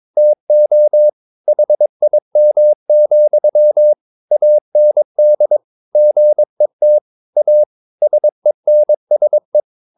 Morse code, heard as a keyed tone, about 22 wpm, 600 Hz. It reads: TO HIM, AND GET A SENSE
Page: Romanen Never af Ken Follet på engelsk.